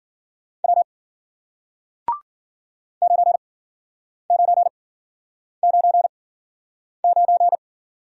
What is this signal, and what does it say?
Morse code, heard as a keyed tone, about 53 wpm, spaced as for 9 wpm, 700 Hz. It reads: W 9999